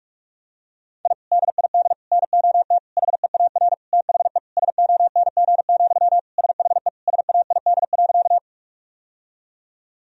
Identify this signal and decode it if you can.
Morse code, heard as a keyed tone, about 46 wpm, 700 Hz. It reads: I DID NOT HEAR THE SONG, SHE SAID.
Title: Sentences from Top 1000 Words - Review Part 2 - 45wpm